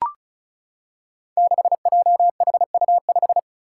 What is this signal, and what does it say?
Morse code, heard as a keyed tone, about 36 wpm, 700 Hz. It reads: 6JHU5
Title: Five Challenging Letter-Number Combinations 35wpm